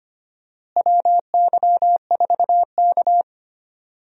Morse code, heard as a keyed tone, about 25 wpm, 700 Hz. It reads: WY4K
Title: USA Call Signs - 4 Characters - Repeated 2x - 25wpm